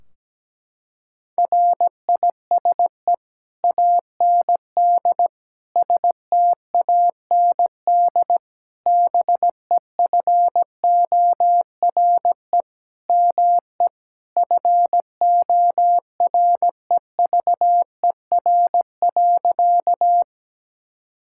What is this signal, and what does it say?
Morse code, heard as a keyed tone, about 17 wpm, 700 Hz. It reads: RISE AND STAND BEFORE ME FOREVER.
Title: Sentences from Top 1600 Words 17wpm